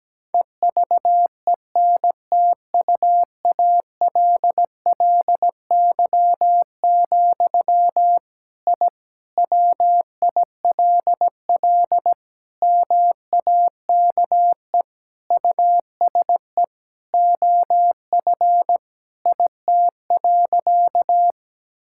Morse code, heard as a keyed tone, about 17 wpm, 700 Hz. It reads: EVENTUALLY, I WILL MAKE USE OF IT.
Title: Sentences from Top 1000 Words - Review Part 6 - 17wpm